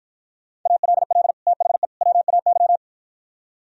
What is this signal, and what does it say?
Morse code, heard as a keyed tone, about 52 wpm, 700 Hz. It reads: ALL THE WAY